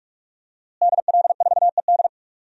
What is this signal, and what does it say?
Morse code, heard as a keyed tone, about 46 wpm, 700 Hz. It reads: DL4ED
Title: DXCC-100 Call Signs - 5 Characters - Repeated 2x - 45wpm